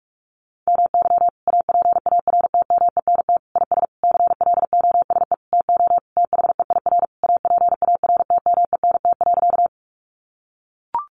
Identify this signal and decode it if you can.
Morse code, heard as a keyed tone, about 44 wpm, 700 Hz. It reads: MY APARTMENT IS CLOSE TO THEIR APARTMENT.
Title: Sentences from Top 900 Words 45wpm